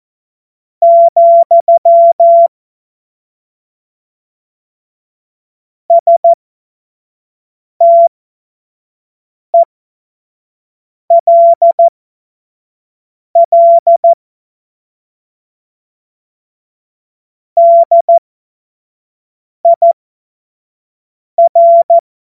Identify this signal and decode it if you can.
Morse code, heard as a keyed tone, about 14 wpm, spaced as for 5 wpm, 680 Hz. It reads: , STELL DIR
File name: Witz-05-August-2025-25-F70